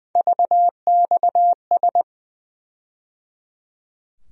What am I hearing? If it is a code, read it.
Morse code, heard as a keyed tone, about 20 wpm, 700 Hz. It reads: VXS
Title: USA Call Sign Postfixes - 3 Letters 20wpm